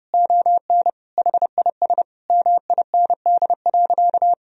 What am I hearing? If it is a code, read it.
Morse code, heard as a keyed tone, about 30 wpm, 700 Hz. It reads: ON HIS MIND.
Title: Sentences from Top 1000 Words - Review Part 2 - 30wpm